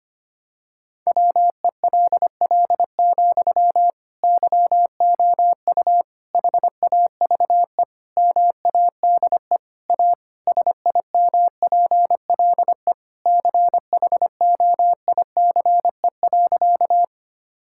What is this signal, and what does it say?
Morse code, heard as a keyed tone, about 25 wpm, 700 Hz. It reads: WELL, YOU HAVE MADE A SIMPLE CHOICE.
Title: Sentences from Top 1000 Words - Review Part 6 - 25wpm